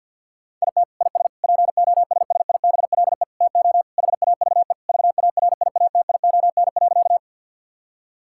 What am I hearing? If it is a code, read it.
Morse code, heard as a keyed tone, about 50 wpm, 700 Hz. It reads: IT IS POSSIBLE TO HAVE VARIATION.
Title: Sentences from Top 1600 Words 50wpm